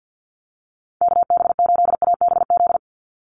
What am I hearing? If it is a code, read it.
Morse code, heard as a keyed tone, about 50 wpm, 700 Hz. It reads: KB8ABZ
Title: USA Call Signs 50wpm